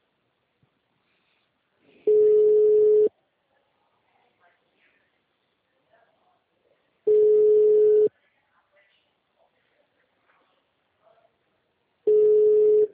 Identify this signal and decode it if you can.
Ringback tone (Europe, 425 Hz), with speech.